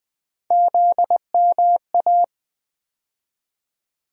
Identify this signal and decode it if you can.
Morse code, heard as a keyed tone, about 20 wpm, 700 Hz. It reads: ZMA
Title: USA Call Sign Postfixes - 3 Letters 20wpm